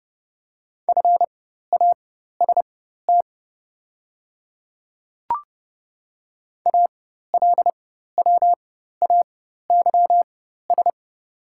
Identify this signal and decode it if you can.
Morse code, heard as a keyed tone, about 30 wpm, spaced as for 14 wpm, 690 Hz. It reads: FAST ALWAYS
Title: 4x Character Spacing - Top 500 Words - 30wpm